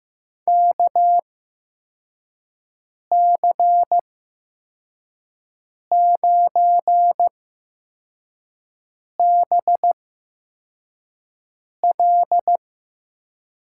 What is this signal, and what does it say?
Morse code, heard as a keyed tone, about 15 wpm, spaced as for 4 wpm, 700 Hz. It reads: KC9BL
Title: USA Call Signs - 8x Character Spacing - 15wpm